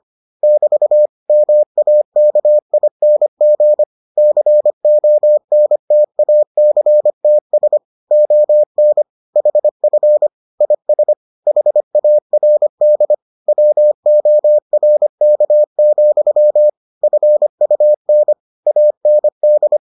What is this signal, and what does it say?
Morse code, heard as a keyed tone, about 25 wpm, 600 Hz. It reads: <BT> MAKING CONTACTS ON HF IS HARD WORK, FUN AND